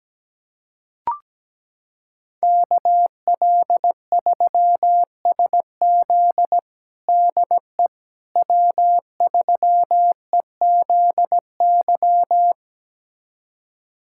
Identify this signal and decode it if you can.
Morse code, heard as a keyed tone, about 17 wpm, 700 Hz. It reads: KL3SZ DE W3EZY
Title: USA Callsign DE Callsign - 5 Characters - 17wpm